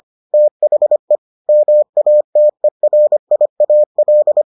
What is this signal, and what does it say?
Morse code, heard as a keyed tone, about 25 wpm, 600 Hz. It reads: THE MATERIAL